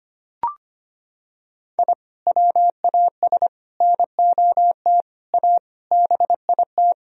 Morse code, heard as a keyed tone, about 25 wpm, 700 Hz. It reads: I WAS NOT A BIT
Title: Sentences from Top 1000 Words - Review Part 4 - 25wpm